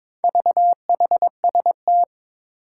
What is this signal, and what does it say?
Morse code, heard as a keyed tone, about 22 wpm, 700 Hz. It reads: VHST
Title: Four Challenging Letter-Number Combinations 22wpm